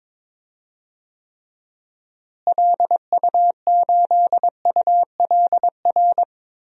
Morse code, heard as a keyed tone, about 22 wpm, 700 Hz. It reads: LU8ULR
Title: DXCC-100 Call Signs - 6 Characters - Repeated 3x - 22wpm